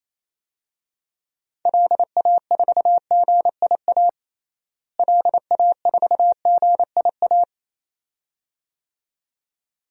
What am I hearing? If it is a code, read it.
Morse code, heard as a keyed tone, about 28 wpm, 700 Hz. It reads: LA4GIA LA4GIA
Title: DXCC-100 Call Signs - 6 Characters - Repeated 2x - 28wpm